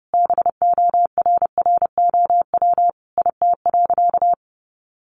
Morse code, heard as a keyed tone, about 30 wpm, 700 Hz. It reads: BORROW IT.
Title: Sentences from Top 1500 Words 30wpm